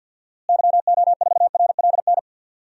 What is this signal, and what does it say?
Morse code, heard as a keyed tone, about 50 wpm, 700 Hz. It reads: YO4RLN
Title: DXCC-100 Call Signs - 6 Characters - Repeated 2x - 50wpm